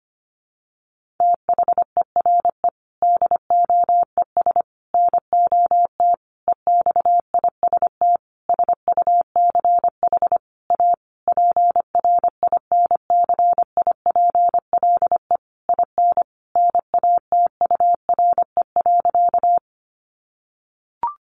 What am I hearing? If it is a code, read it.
Morse code, heard as a keyed tone, about 25 wpm, 700 Hz. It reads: THERE DOES NOT EXIST SUCH A PRINCIPLE IN NATURE.
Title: Sentences from Top 1400 Words 25wpm